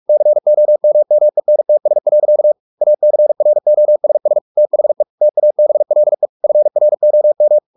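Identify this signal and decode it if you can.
Morse code, heard as a keyed tone, about 44 wpm, 600 Hz. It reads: COMMENTS. ACROSS THE TABLE FROM
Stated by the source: Morse practice at 45 wpm, the novel Never by Ken Follett in English, from CW med Gnister.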